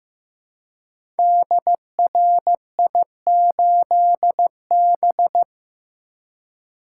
Morse code, heard as a keyed tone, about 15 wpm, 700 Hz. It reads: DRI8B
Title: Code Groups - 5 Letter:Numbers - 15wpm